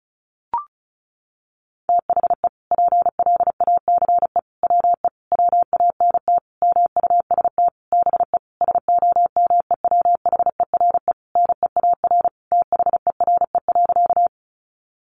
Morse code, heard as a keyed tone, about 34 wpm, 700 Hz. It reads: THE PLACE WE WANT MUST BE SOMEWHERE NEAR THERE.
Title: Sentences from Top 1400 Words 35wpm